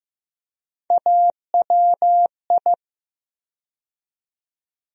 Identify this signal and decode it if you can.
Morse code, heard as a keyed tone, about 15 wpm, 700 Hz. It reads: AWI
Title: USA Call Sign Postfixes - 3 Letters 15wpm